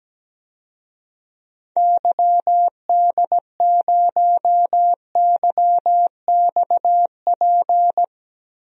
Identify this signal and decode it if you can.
Morse code, heard as a keyed tone, about 17 wpm, 700 Hz. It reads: YD0YXP
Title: DXCC-100 Call Signs - 6 Characters - Repeated 2x - 17wpm